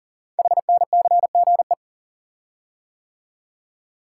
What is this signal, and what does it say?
Morse code, heard as a keyed tone, about 39 wpm, 700 Hz. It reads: HNCGE
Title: Code Groups - 5 Letters - 40wpm